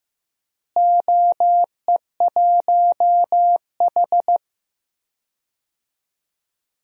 Morse code, heard as a keyed tone, about 15 wpm, 700 Hz. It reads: OE1H
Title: DXCC-100 Call Signs - 4 Characters - Repeated 3x - 15wpm